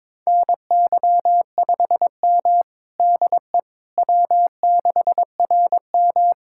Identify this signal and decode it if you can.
Morse code, heard as a keyed tone, about 22 wpm, 700 Hz. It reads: NY5M DE W6RM
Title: USA Callsign DE Callsign - 4 Characters - 22wpm